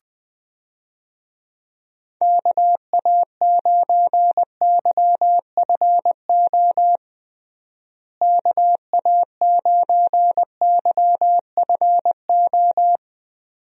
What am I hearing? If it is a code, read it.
Morse code, heard as a keyed tone, about 21 wpm, 700 Hz. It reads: KA9YFO KA9YFO
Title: USA Call Signs - 6 Characters - Repeated 2x - 20wpm